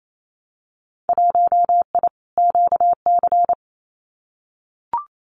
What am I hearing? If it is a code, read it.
Morse code, heard as a keyed tone, about 28 wpm, 700 Hz. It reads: 1I QC
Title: Winter Field Day Exchanges 28wpm